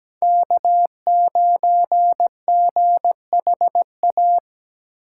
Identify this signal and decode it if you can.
Morse code, heard as a keyed tone, about 17 wpm, 700 Hz. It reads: K9GHA
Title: USA Call Signs - 5 Characters - 4x Character Spacing - 17wpm